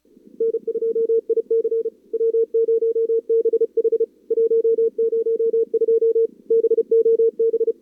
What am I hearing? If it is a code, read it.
Morse code, heard as a keyed tone, about 36 wpm, 440 Hz. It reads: N2IC W0BH 102 BOB
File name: CWSprint / Feb2022 / k5tr-feb-na-sprint-cw-k5ot -0000